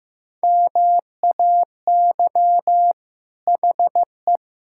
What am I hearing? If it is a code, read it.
Morse code, heard as a keyed tone, about 15 wpm, 700 Hz. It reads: MAY HE
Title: Sentences from Top 1000 Words - Review Part 6 - 15wpm